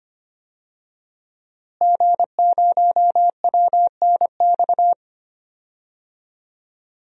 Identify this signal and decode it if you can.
Morse code, heard as a keyed tone, about 25 wpm, 700 Hz. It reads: G0WNX
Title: DXCC-100 Call Signs - 5 Characters - Repeated 3x - 25wpm